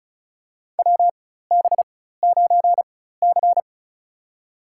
Morse code, heard as a keyed tone, about 35 wpm, spaced as for 16 wpm, 700 Hz. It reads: WB9C
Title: USA Call Signs - 4 Characters - 4x Character Spacing - 35wpm